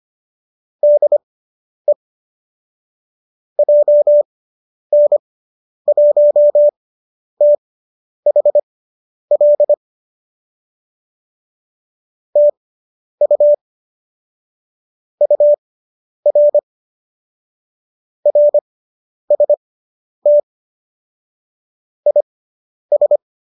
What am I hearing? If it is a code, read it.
Morse code, heard as a keyed tone, about 25 wpm, spaced as for 10 wpm, 590 Hz. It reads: DE JN1THL TU UR RST IS